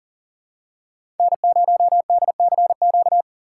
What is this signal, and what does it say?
Morse code, heard as a keyed tone, about 40 wpm, 700 Hz. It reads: N0DCQ
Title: USA Call Signs - 3x Character Spacing - 40wpm